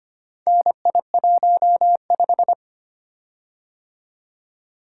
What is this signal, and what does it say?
Morse code, heard as a keyed tone, about 25 wpm, 690 Hz. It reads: NI15
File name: practice / 25wpm / Four Letter-Number Combinations - 25wpm